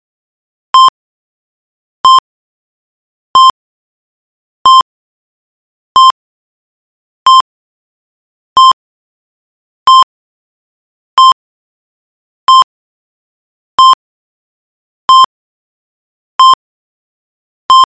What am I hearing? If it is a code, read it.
There is no code.